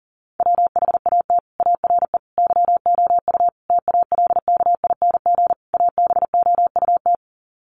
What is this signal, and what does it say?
Morse code, heard as a keyed tone, about 40 wpm, 700 Hz. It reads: WHAT ARE YOU TALKING ABOUT
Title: Sets of 5 Words 40wpm